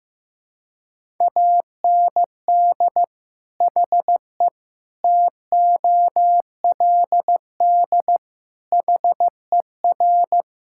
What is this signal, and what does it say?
Morse code, heard as a keyed tone, about 16 wpm, 700 Hz. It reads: AND HE TOLD HER
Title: Sentences from Top 1000 Words - Review Part 5 - 15wpm